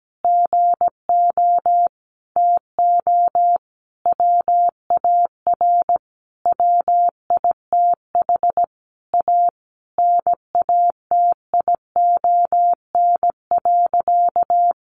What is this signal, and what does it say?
Morse code, heard as a keyed tone, about 17 wpm, 700 Hz. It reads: GO TO WAR WITH A NATION.
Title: Sentences from Top 1200 Words 17wpm